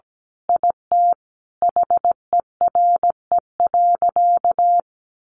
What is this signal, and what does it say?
Morse code, heard as a keyed tone, about 17 wpm, 700 Hz. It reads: IT HERE.
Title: Sentences from Top 1100 Words 17wpm